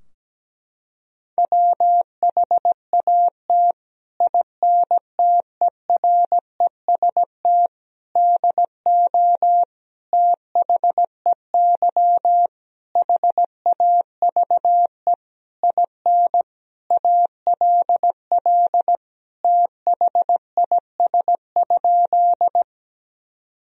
Morse code, heard as a keyed tone, about 17 wpm, 700 Hz. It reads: WHAT INTEREST DO THEY HAVE IN ALL THIS?
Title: Sentences from Top 1000 Words - Review Part 2 - 17wpm